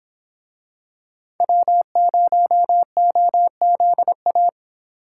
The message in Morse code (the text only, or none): W0OZA